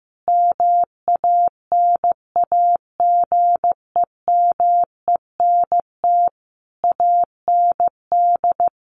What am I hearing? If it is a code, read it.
Morse code, heard as a keyed tone, about 14 wpm, 700 Hz. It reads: MANAGEMENT AND
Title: Sets of 5 Words Encore2 15wpm